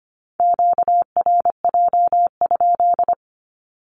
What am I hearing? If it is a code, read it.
Morse code, heard as a keyed tone, about 25 wpm, 700 Hz. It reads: QRJ?
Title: Q-Codes Repeated 3x 25wpm